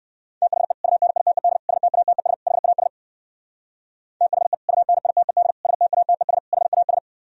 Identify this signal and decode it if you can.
Morse code, heard as a keyed tone, about 63 wpm, 700 Hz. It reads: THE UNITED STATES HAS THE UNITED STATES HAS